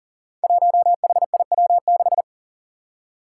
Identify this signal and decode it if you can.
Morse code, heard as a keyed tone, about 40 wpm, 700 Hz. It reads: 1HIW6
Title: Five Challenging Letter-Number Combinations 40wpm